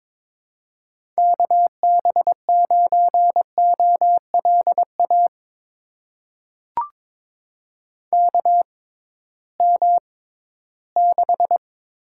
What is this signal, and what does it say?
Morse code, heard as a keyed tone, about 22 wpm, spaced as for 14 wpm, 700 Hz. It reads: KB9OLA KM6
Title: USA Call Signs - 6 Characters - 6x Character Spacing - 22wpm